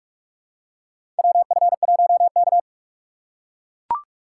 Morse code, heard as a keyed tone, about 42 wpm, 700 Hz. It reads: WF1K
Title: USA Call Signs - 4 Characters - 4x Character Spacing - 45wpm